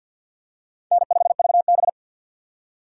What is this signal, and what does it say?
Morse code, heard as a keyed tone, about 51 wpm, 700 Hz. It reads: N5VB
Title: USA Call Signs - 4 Characters - 2x Character Spacing - 50wpm